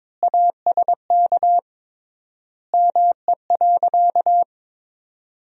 Morse code, heard as a keyed tone, about 22 wpm, 700 Hz. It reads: ASK ME.
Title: 3x Word-Spacing - Sentences from Top 500 Words - 22wpm